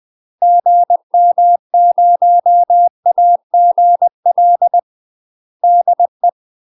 Morse code, heard as a keyed tone, about 20 wpm, 700 Hz. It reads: GM0AGL DE
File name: UK MASTER QSOs TRAINING / 20WPM / UK MASTER QSOs 0014